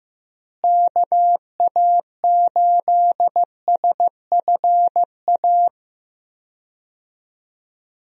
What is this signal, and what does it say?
Morse code, heard as a keyed tone, about 16 wpm, 700 Hz. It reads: KA8SFA
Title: USA Call Signs - 6 Characters - Repeated 2x - 15wpm